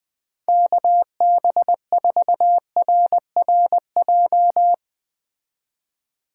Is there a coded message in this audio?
Yes: Morse code from 0.5 to 4.7 s, heard as a keyed tone, about 20 wpm, 700 Hz. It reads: KB4RRJ